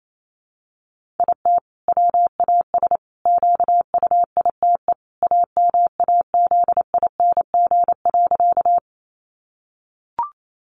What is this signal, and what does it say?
Morse code, heard as a keyed tone, about 28 wpm, 700 Hz. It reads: IT WAS QUITE AMAZING.
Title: Sentences from Top 1500 Words 28wpm